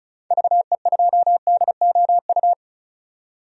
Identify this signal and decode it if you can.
Morse code, heard as a keyed tone, about 34 wpm, 700 Hz. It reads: VE2DOU
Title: DXCC-100 Call Signs - 6 Characters - 35wpm